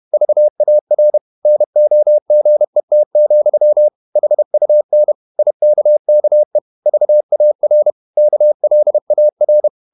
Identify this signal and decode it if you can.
Morse code, heard as a keyed tone, about 31 wpm, 600 Hz. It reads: VAR NOGET, HUN IKKE VAR KLAR
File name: Ferie QRQ 31wpm